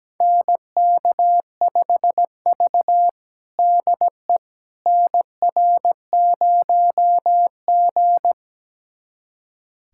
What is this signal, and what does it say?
Morse code, heard as a keyed tone, about 17 wpm, 700 Hz. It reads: NK5V DE NR0G
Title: USA Callsign DE Callsign - 4 Characters - 17wpm